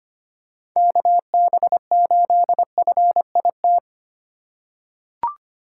Morse code, heard as a keyed tone, about 25 wpm, 700 Hz. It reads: KB8FIT
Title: USA Call Signs - 8x Character Spacing - 25wpm